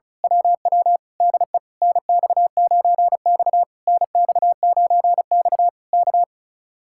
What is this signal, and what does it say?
Morse code, heard as a keyed tone, about 33 wpm, 700 Hz. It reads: WW DE NX9X NX9X K
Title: Speed-Racing Common QSO Phrases - 35wpm